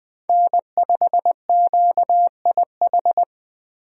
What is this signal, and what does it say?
Morse code, heard as a keyed tone, about 21 wpm, 700 Hz. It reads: N5QIH
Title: USA Call Signs - 5 Characters - 20wpm